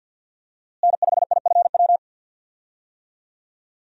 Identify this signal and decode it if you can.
Morse code, heard as a keyed tone, about 50 wpm, 700 Hz. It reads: N5IFW
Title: USA Call Signs - 5 Characters - Repeated 2x - 50wpm